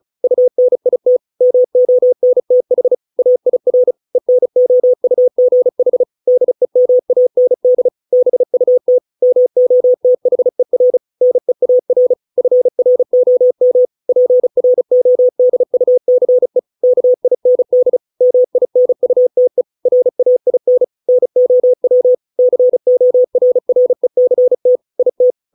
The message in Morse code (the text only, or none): UNIT MONTH AIR ENOUGH DEMAND BUT MOTHER NEAR FROM PRODUCE KIND MINUTE RAIN NOW CORRECT IT